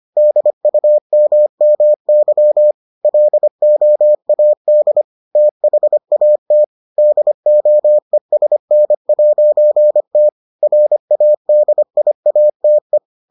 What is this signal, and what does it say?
Morse code, heard as a keyed tone, about 25 wpm, 600 Hz. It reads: DUMMY LOAD THAT DOESN'T RADIATE